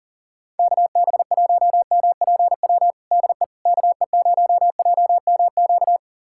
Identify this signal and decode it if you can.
Morse code, heard as a keyed tone, about 40 wpm, 700 Hz. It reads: KB1MPW DE KE0JMQ
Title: USA Callsign DE Callsign - 6 Characters - 40wpm